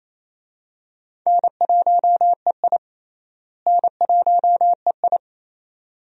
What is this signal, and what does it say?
Morse code, heard as a keyed tone, about 27 wpm, 700 Hz. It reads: N1EI N1EI
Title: USA Call Signs - 4 Characters - Repeated 2x - 28wpm